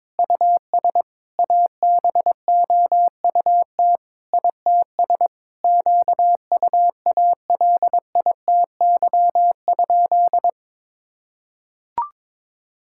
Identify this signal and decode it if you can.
Morse code, heard as a keyed tone, about 22 wpm, 700 Hz. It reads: US ABOUT ITS QUALITY?
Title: Sentences from Top 1000 Words - Review Part 6 - 22wpm